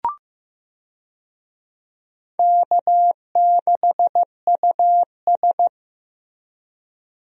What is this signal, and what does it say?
Morse code, heard as a keyed tone, about 15 wpm, 700 Hz. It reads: K6US
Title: USA Call Signs - 4 Characters - Repeated 2x - 15wpm